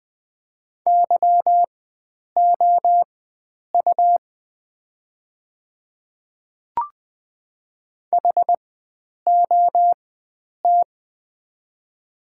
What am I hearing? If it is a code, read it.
Morse code, heard as a keyed tone, about 20 wpm, spaced as for 9 wpm, 690 Hz. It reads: YOU HOT
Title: 4x Character Spacing - Top 100 Words - 20wpm